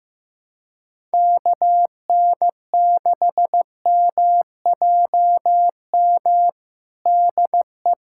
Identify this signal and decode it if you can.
Morse code, heard as a keyed tone, about 15 wpm, 700 Hz. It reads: KN6MJM DE
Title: USA Callsign DE Callsign - 6 Characters - 15wpm